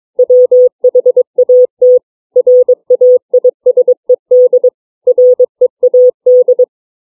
Morse code, heard as a keyed tone, about 22 wpm, 500 Hz. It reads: WHAT RAISED READ
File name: CW / print letters